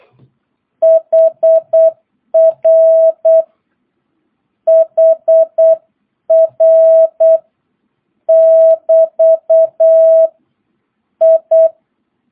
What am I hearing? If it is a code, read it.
Morse code, heard as a keyed tone, about 8 wpm, 650 Hz. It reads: HR HR <BT> I
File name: jh8cbh / 2022cw